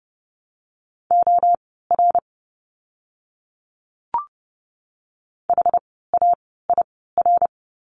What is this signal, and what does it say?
Morse code, heard as a keyed tone, about 30 wpm, spaced as for 17 wpm, 700 Hz. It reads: OR HAIR
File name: Lesson 17f - Letter u - Words 30wpm